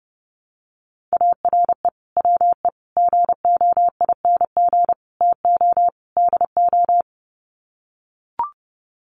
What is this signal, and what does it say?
Morse code, heard as a keyed tone, about 30 wpm, 700 Hz. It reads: ARE WE GOING TO DO
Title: Sets of 5 Words 30wpm